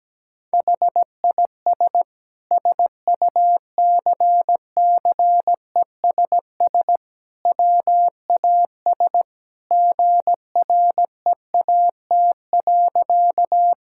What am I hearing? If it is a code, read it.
Morse code, heard as a keyed tone, about 17 wpm, 700 Hz. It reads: HIS SUCCESS WAS GREAT.
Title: Sentences from Top 1000 Words - Review Part 6 - 17wpm